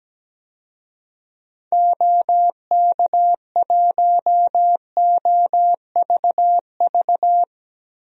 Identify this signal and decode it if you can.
Morse code, heard as a keyed tone, about 17 wpm, 700 Hz. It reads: OK1OVV
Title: DXCC-100 Call Signs - 6 Characters - Repeated 2x - 17wpm